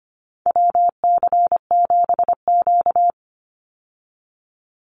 Morse code, heard as a keyed tone, about 25 wpm, 700 Hz. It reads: WC7Q
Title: Call Signs - 25wpm